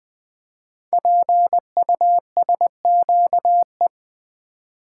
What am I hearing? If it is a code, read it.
Morse code, heard as a keyed tone, about 20 wpm, 700 Hz. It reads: PUSQE
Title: Code Groups - 5 Letters - 20wpm